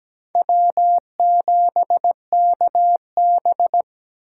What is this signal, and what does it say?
Morse code, heard as a keyed tone, about 17 wpm, 700 Hz. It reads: W7KB
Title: USA Call Signs - 4 Characters - 3x Character Spacing - 17wpm